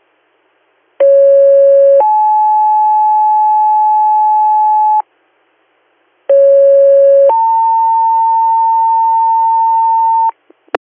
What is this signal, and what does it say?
Two-tone page, heard as a one-second tone, then a three-second tone, about 550 then 860 Hz, about 550 then 900 Hz, with noise. It is heard from 1.0 to 10.3 s.